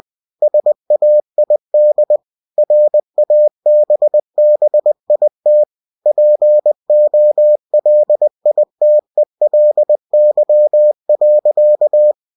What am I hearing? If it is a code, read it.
Morse code, heard as a keyed tone, about 20 wpm, 600 Hz. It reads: SAID RABBIT POLITELY.
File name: cw-books / winnie-the-pooh / winnie-20wpm-page-0015